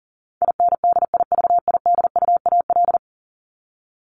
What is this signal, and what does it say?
Morse code, heard as a keyed tone, about 40 wpm, 700 Hz. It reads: INDIVIDUAL
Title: Top 501-750 Words 40wpm